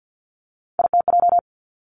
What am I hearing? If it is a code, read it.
Morse code, heard as a keyed tone, about 50 wpm, 700 Hz. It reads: ITJ
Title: Three Challenging Letter-Number Combinations 50wpm